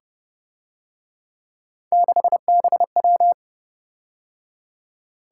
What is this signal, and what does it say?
Morse code, heard as a keyed tone, about 29 wpm, 700 Hz. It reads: 6BW
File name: Speed-Racing - Three Letter-Number Combinations - 25wpm